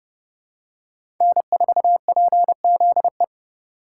Morse code, heard as a keyed tone, about 30 wpm, 700 Hz. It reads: N4PZE
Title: USA Call Signs - 4x Character Spacing - 30wpm